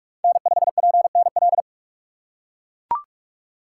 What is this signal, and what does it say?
Morse code, heard as a keyed tone, about 42 wpm, 700 Hz. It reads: N5PNL
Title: USA Call Signs - 2x Character Spacing - 45wpm